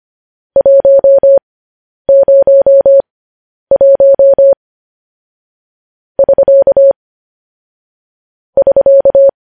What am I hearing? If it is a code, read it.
Morse code, heard as a keyed tone, about 25 wpm, spaced as for 10 wpm, 550 Hz. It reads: 101 <SK> <SK>
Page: News Headlines in Morse Code at 10 WPM